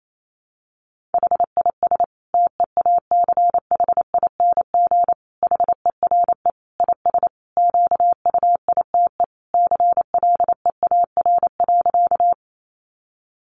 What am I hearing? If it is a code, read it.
Morse code, heard as a keyed tone, about 28 wpm, 700 Hz. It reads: HIS TEACHING HERE IS QUITE CLEAR.
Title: Sentences from Top 1100 Words 28wpm